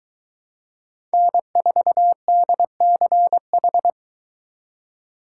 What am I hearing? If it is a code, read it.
Morse code, heard as a keyed tone, about 23 wpm, 700 Hz. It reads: N4DCH